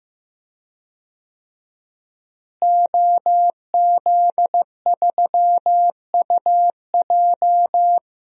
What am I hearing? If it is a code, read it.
Morse code, heard as a keyed tone, about 15 wpm, 700 Hz. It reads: OZ3UJ
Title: DXCC-100 Call Signs - 5 Characters - Repeated 2x - 15wpm